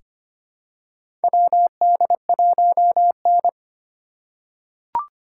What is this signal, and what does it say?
Morse code, heard as a keyed tone, about 25 wpm, 700 Hz. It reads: WD1N
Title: USA Call Signs - 4 Characters - 4x Character Spacing - 25wpm